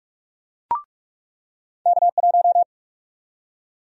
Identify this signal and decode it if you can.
Morse code, heard as a keyed tone, about 48 wpm, 700 Hz. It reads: K1
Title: USA Call Sign Prefixes - 45wpm